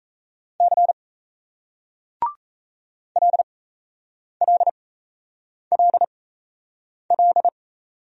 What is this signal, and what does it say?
Morse code, heard as a keyed tone, about 35 wpm, spaced as for 8 wpm, 700 Hz. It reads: C LLLL
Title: Speed-Racing Single Letters 28wpm